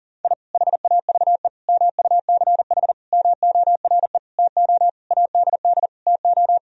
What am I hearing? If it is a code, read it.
Morse code, heard as a keyed tone, about 40 wpm, 700 Hz. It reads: I HAVE MUCH MORE TO ADD TO
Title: Sentences from Top 1000 Words - Review Part 1 - 40wpm